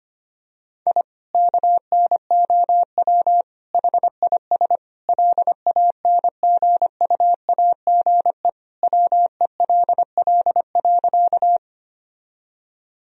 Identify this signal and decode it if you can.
Morse code, heard as a keyed tone, about 25 wpm, 700 Hz. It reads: I KNOW HIS LANGUAGE WELL.
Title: Sentences from Top 1000 Words - Review Part 3 - 25wpm